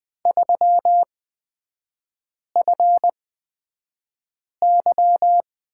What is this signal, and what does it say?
Morse code, heard as a keyed tone, about 20 wpm, spaced as for 5 wpm, 700 Hz. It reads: 3FY
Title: 243 Lesson 39e - Letter z - Three Characters 20wpm